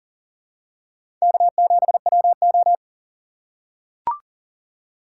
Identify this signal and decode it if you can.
Morse code, heard as a keyed tone, about 40 wpm, 700 Hz. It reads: K7WO
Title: USA Call Signs - 4 Characters - 4x Character Spacing - 40wpm